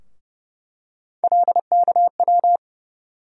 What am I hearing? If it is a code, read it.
Morse code, heard as a keyed tone, about 32 wpm, 700 Hz. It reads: LKW
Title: USA Call Sign Postfixes - 3 Letters 30wpm